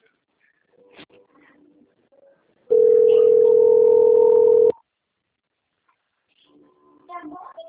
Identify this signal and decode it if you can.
Ringback tone (North America), 440 + 480 Hz.